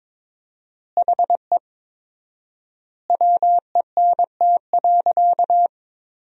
Morse code, heard as a keyed tone, about 22 wpm, 700 Hz. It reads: HE WENT.
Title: 4x Word-Spacing - Sentences from Top 200 Words - 22wpm